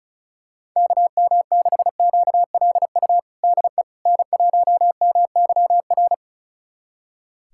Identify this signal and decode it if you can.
Morse code, heard as a keyed tone, about 36 wpm, 700 Hz. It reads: KM6QLU DE N1MYR
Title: USA Callsign DE Callsign 35wpm